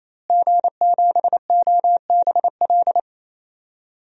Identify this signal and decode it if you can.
Morse code, heard as a keyed tone, about 28 wpm, 700 Hz. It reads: G7OBL
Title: DXCC-100 Call Signs - 5 Characters - Repeated 3x - 28wpm